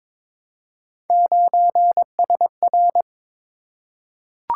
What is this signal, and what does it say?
Morse code, heard as a keyed tone, about 22 wpm, 700 Hz. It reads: 9SR
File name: practice / 22wpm / Three Letter-Number Combinations - 22wpm